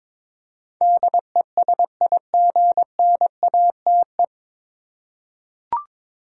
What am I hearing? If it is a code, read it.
Morse code, heard as a keyed tone, about 22 wpm, 700 Hz. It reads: DESIGNATE
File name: 9 Letter Words - First 3 Letters Given - 22wpm